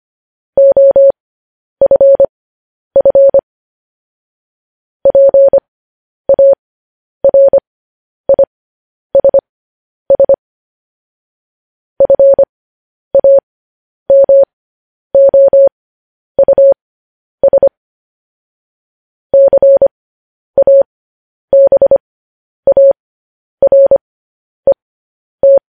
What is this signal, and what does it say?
Morse code, heard as a keyed tone, about 25 wpm, spaced as for 10 wpm, 550 Hz. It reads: OFF PARISS FAMOUS CABARET